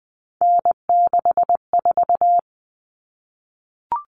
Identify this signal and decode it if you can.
Morse code, heard as a keyed tone, about 20 wpm, 700 Hz. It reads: N64
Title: Common Cut Numbers - 3-Digit 20wpm